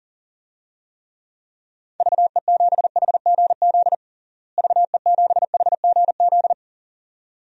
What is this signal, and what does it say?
Morse code, heard as a keyed tone, about 40 wpm, 700 Hz. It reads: VE7HGZ VE7HGZ
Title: DXCC-100 Call Signs - 6 Characters - Repeated 2x - 40wpm